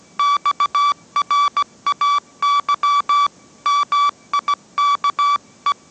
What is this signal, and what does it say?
Morse code, heard as a keyed tone, about 17 wpm, 1150 Hz. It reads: XRAY MIKE
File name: mrx / mrx080 18wpm